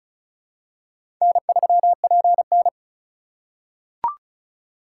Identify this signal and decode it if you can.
Morse code, heard as a keyed tone, about 36 wpm, 700 Hz. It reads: N3PN
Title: USA Call Signs - 4 Characters - 4x Character Spacing - 35wpm